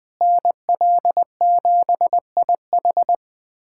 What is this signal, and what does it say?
Morse code, heard as a keyed tone, about 20 wpm, 700 Hz. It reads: NL7IH
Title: USA Callsign DE Callsign 20wpm